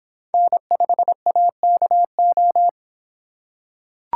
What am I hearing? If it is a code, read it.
Morse code, heard as a keyed tone, about 26 wpm, 700 Hz. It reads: N5AKO